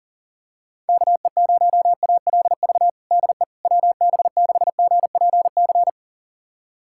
Morse code, heard as a keyed tone, about 37 wpm, 690 Hz. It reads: KE0ALV DE WB6GPC
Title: USA Callsign DE Callsign - 6 Characters - 40wpm